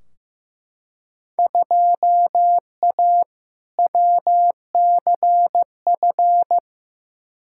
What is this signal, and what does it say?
Morse code, heard as a keyed tone, about 15 wpm, 700 Hz. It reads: 2A WCF
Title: Field Day Exchanges Encore - 15wpm